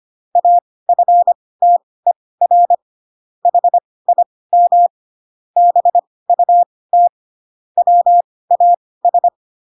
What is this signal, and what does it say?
Morse code, heard as a keyed tone, about 25 wpm, spaced as for 18 wpm, 700 Hz. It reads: AFTER HIM BUT WAS
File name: Pinocchio ch3 12 25 ik1hge